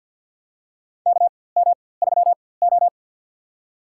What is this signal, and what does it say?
Morse code, heard as a keyed tone, about 47 wpm, spaced as for 23 wpm, 700 Hz. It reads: KM3O
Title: USA Call Signs - 4x Character Spacing - 50wpm